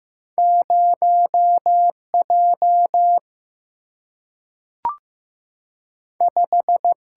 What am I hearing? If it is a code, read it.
Morse code, heard as a keyed tone, about 15 wpm, 700 Hz. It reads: 0 J 5
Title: Single-Letter Single-Number 15wpm